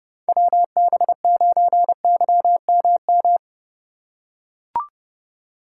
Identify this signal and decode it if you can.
Morse code, heard as a keyed tone, about 32 wpm, 700 Hz. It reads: WB9YMM
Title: USA Call Signs - 6 Characters - 4x Character Spacing - 30wpm